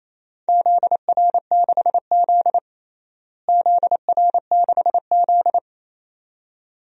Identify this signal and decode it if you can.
Morse code, heard as a keyed tone, about 28 wpm, 700 Hz. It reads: ZR6Z ZR6Z
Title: DXCC-100 Call Signs - 4 Characters - Repeated 2x - 28wpm